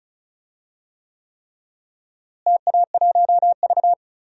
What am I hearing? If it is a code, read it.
Morse code, heard as a keyed tone, about 34 wpm, 700 Hz. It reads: TA1V
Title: DXCC-100 Call Signs - 4 Characters - Repeated 3x - 35wpm